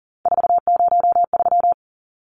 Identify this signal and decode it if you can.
Morse code, heard as a keyed tone, about 40 wpm, 700 Hz. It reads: 403
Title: Numbers - 3-Digit 40wpm